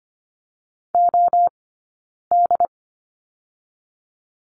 Morse code, heard as a keyed tone, about 25 wpm, spaced as for 9 wpm, 690 Hz. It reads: OD
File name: Lesson 15d - Number 5 - Two Characters 25wpm